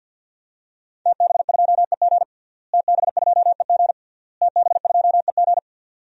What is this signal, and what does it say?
Morse code, heard as a keyed tone, about 50 wpm, 700 Hz. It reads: TB2EG TB2EG TB2EG
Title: DXCC-100 Call Signs - 5 Characters - Repeated 3x - 50wpm